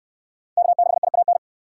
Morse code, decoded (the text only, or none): K6IAN